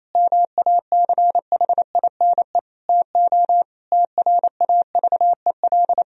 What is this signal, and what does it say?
Morse code, heard as a keyed tone, about 28 wpm, 700 Hz. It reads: MACHINE TO TRAVEL
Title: Sentences from Top 1000 Words - Review Part 2 - 28wpm